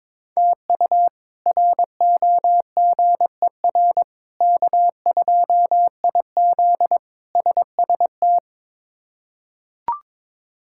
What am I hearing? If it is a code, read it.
Morse code, heard as a keyed tone, about 22 wpm, 700 Hz. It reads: TU ROGER K2IZ SST
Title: K1USN SST QSO Practice - 22wpm